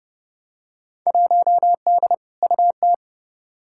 Morse code, heard as a keyed tone, about 30 wpm, 700 Hz. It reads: 1D UT
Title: Field Day Exchanges - 30wpm